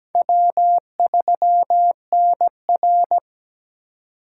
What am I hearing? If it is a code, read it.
Morse code, heard as a keyed tone, about 20 wpm, 700 Hz. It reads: W3NR
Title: Speed Racing - USA 1x2 Call Signs 17wpm